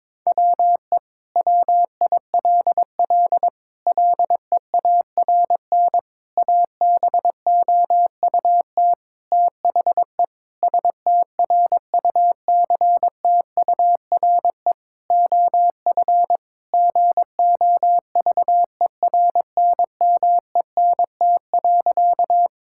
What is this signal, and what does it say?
Morse code, heard as a keyed tone, about 22 wpm, 700 Hz. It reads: WE WILL LEARN ABOUT THE STRUCTURE OF GOVERNMENT.
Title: Sentences from Top 1000 Words - Review Part 6 - 22wpm